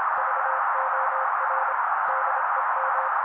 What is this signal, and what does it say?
Morse code, heard as a keyed tone, about 27 wpm, 530 Hz. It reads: VOR DEM
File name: daily / 25wpm noise